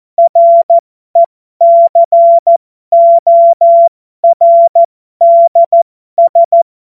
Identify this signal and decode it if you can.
Morse code, heard as a keyed tone, about 14 wpm, spaced as for 12 wpm, 680 Hz. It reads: RECORDS
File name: Morsekurs / Witze